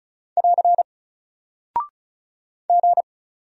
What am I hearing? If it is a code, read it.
Morse code, heard as a keyed tone, about 34 wpm, 700 Hz. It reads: <AR> G
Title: Single Character (Letter:Number:Punctuation:Prosign) Alt 35wpm